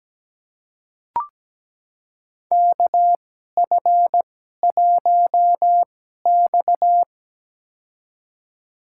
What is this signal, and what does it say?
Morse code, heard as a keyed tone, about 17 wpm, spaced as for 12 wpm, 700 Hz. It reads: KF1X
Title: USA Call Signs - 4 Characters - 2x Character Spacing - 17wpm